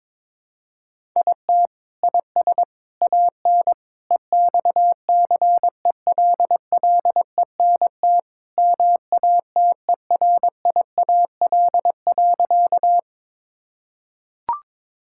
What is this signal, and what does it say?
Morse code, heard as a keyed tone, about 22 wpm, 700 Hz. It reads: IT IS AN EXCELLENT MATERIAL.
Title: Sentences from Top 1000 Words - Review Part 5 - 22wpm